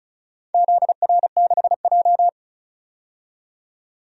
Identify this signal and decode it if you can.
Morse code, heard as a keyed tone, about 36 wpm, 700 Hz. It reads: ZR6J
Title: DXCC-100 Call Signs - 4 Characters - 35wpm